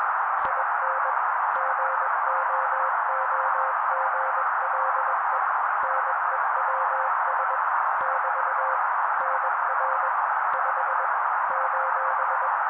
Morse code, heard as a keyed tone, about 21 wpm, 530 Hz. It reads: IN GOOGLE NEWS <BT> NR 5 :
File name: daily / 20wpm noise